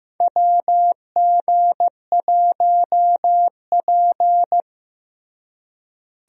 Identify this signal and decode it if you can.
Morse code, heard as a keyed tone, about 15 wpm, 700 Hz. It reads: WG1P
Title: USA Call Signs - 4 Characters - Repeated 2x - 15wpm